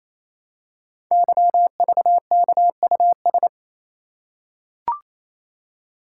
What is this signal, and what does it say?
Morse code, heard as a keyed tone, about 27 wpm, 700 Hz. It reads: YVKUS
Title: Code Groups - 5 Letters - 28wpm